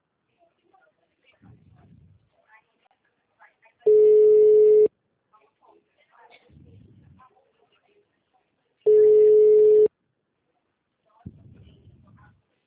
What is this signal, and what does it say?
Ringback tone (Europe, 425 Hz), with speech.